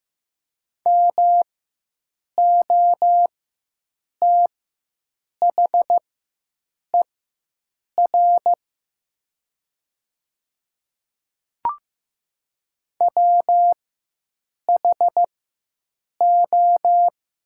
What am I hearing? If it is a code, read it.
Morse code, heard as a keyed tone, about 15 wpm, spaced as for 7 wpm, 700 Hz. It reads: MOTHER WHO
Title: 4x Character Spacing - Top 200 Words - 15wpm